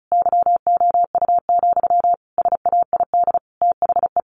Morse code, heard as a keyed tone, about 34 wpm, 700 Hz. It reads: YOU, SAID THE
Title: Sentences from Top 600 Words 35wpm